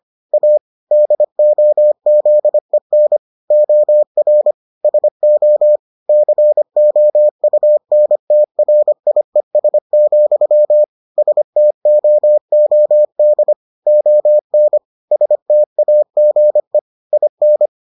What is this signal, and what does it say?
Morse code, heard as a keyed tone, about 25 wpm, 600 Hz. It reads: A DOZEN OR SO COUNTRIES, STOOD ON STAGE IN